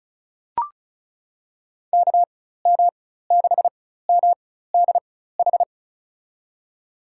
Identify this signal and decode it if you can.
Morse code, heard as a keyed tone, about 36 wpm, spaced as for 16 wpm, 700 Hz. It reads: KM6MDH
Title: USA Call Signs - 6 Characters - 4x Character Spacing - 35wpm